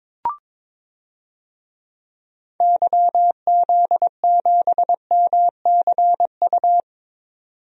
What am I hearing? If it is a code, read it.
Morse code, heard as a keyed tone, about 22 wpm, 700 Hz. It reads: YZ7MCU
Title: DXCC-100 Call Signs - 6 Characters - 22wpm